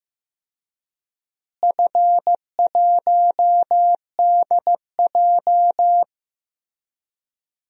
Morse code, heard as a keyed tone, about 16 wpm, 700 Hz. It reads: F1DJ
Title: DXCC-100 Call Signs - 4 Characters - Repeated 3x - 15wpm